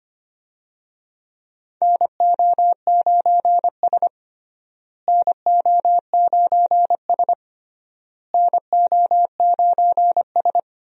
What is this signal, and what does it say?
Morse code, heard as a keyed tone, about 25 wpm, 700 Hz. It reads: NO9S NO9S NO9S
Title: USA Call Signs - 4 Characters - Repeated 3x - 25wpm